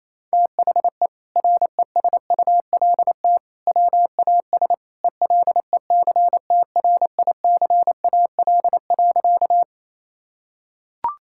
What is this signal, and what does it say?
Morse code, heard as a keyed tone, about 27 wpm, 700 Hz. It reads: THE RESULT WAS ELECTRICAL.
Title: Sentences from Top 1000 Words - Review Part 6 - 28wpm